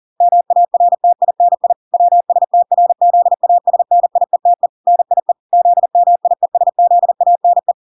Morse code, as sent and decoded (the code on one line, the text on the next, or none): -- .- .-. - .. -. .. / .-- ... - .-. --.. .- ... -. .. - . / -. .. / --.. -- .. ... --.. .- -. .
MARTINI WSTRZASNIETE NIE ZMIESZANE